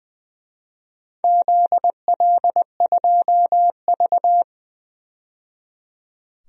Morse code, heard as a keyed tone, about 20 wpm, 700 Hz. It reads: ZL2V
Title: DXCC-100 Call Signs - 4 Characters - 20wpm